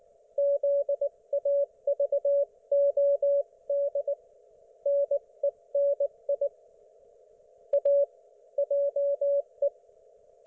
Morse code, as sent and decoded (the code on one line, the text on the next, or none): --.. .- ...- --- -.. / -. -. .. / .- / .--- .
ZAVOD NENI A JE